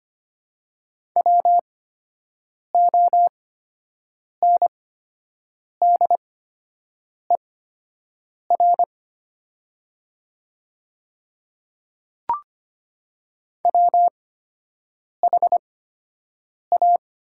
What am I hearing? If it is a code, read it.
Morse code, heard as a keyed tone, about 25 wpm, spaced as for 7 wpm, 700 Hz. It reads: WONDER WHA
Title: 8x Character Spacing - Top 500 Words - 25wpm